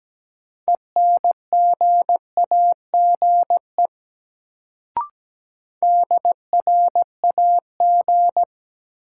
Morse code, heard as a keyed tone, about 17 wpm, 700 Hz. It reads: ENGAGE DRAG
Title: Sentences from Top 1500 Words - New Word Drill Alt - 17wpm